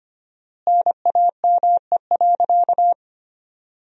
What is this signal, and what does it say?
Morse code, heard as a keyed tone, about 25 wpm, 700 Hz. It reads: NAME.
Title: 4x Word-Spacing - Sentences from Top 200 Words - 25wpm